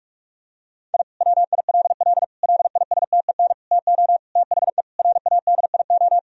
Morse code, heard as a keyed tone, about 42 wpm, 700 Hz. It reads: I WILL LISTEN TO THE RADIO
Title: Sentences from Top 1000 Words - Review Part 3 - 45wpm